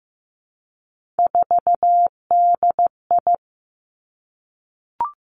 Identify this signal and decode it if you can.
Morse code, heard as a keyed tone, about 15 wpm, 700 Hz. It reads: 4DI
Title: Three Challenging Letter-Number Combinations 15wpm